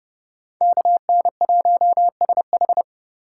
Morse code, heard as a keyed tone, about 31 wpm, 700 Hz. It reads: KN1SH